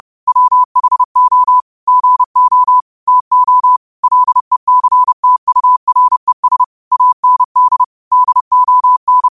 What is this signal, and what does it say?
Morse code, heard as a keyed tone, about 30 wpm, 1000 Hz. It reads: WHO GO TO LECTURES AND DON